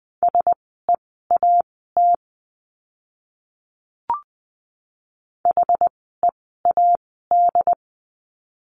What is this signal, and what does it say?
Morse code, heard as a keyed tone, about 19 wpm, spaced as for 14 wpm, 700 Hz. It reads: SEAT HEAD
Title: 068 Lesson 12f - Letter d - Words 20wpm